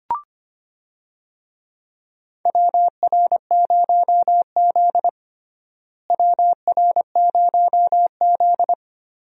Morse code, heard as a keyed tone, about 25 wpm, 700 Hz. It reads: WR0Z WR0Z
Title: USA Call Signs - 4 Characters - Repeated 2x - 25wpm